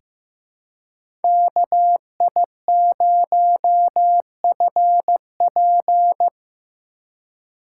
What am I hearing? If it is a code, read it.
Morse code, heard as a keyed tone, about 15 wpm, 700 Hz. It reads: KI0FP
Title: USA Call Signs - 5 Characters - 3x Character Spacing - 15wpm